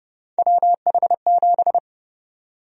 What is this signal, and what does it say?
Morse code, heard as a keyed tone, about 30 wpm, 700 Hz. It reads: WH7
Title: USA Call Sign Prefixes - 30wpm